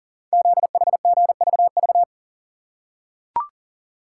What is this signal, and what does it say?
Morse code, heard as a keyed tone, about 42 wpm, 690 Hz. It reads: ZHGVV
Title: Code Groups - 5 Letters - 40wpm